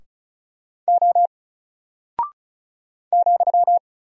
Morse code, heard as a keyed tone, about 36 wpm, 700 Hz. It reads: O ,
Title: Single Character (Letter:Number:Punctuation:Prosign) 35wpm